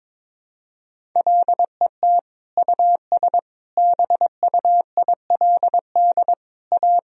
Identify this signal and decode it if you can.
Morse code, heard as a keyed tone, about 22 wpm, 700 Hz. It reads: LET US BUILD A
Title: Sentences from Top 1000 Words - Review Part 1 - 22wpm